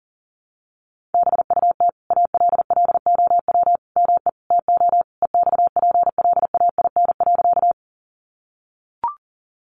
Morse code, heard as a keyed tone, about 40 wpm, 700 Hz. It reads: BUT ALLOW ME TO EXPLAIN.
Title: Sentences from Top 1100 Words 40wpm